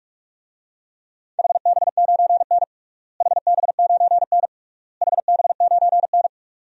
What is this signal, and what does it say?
Morse code, heard as a keyed tone, about 41 wpm, 700 Hz. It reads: HB9N HB9N HB9N
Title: DXCC-100 Call Signs - 4 Characters - Repeated 3x - 45wpm